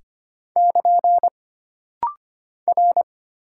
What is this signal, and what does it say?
Morse code, heard as a keyed tone, about 25 wpm, 700 Hz. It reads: <KN> R
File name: Single Character (Letter:Number:Punctuation:Prosign) - Rapid-Fire - 25wpm